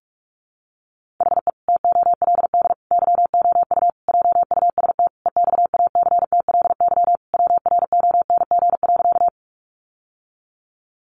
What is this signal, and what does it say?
Morse code, heard as a keyed tone, about 44 wpm, 700 Hz. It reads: HE TOLD YOU JUST EXACTLY WRONG.
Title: Sentences from Top 900 Words 45wpm